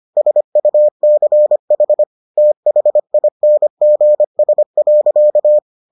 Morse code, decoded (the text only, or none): SUCH THINGS.